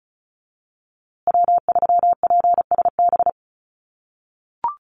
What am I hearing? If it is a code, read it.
Morse code, heard as a keyed tone, about 34 wpm, 700 Hz. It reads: W3PSB
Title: USA Call Signs Repeated 2x 35wpm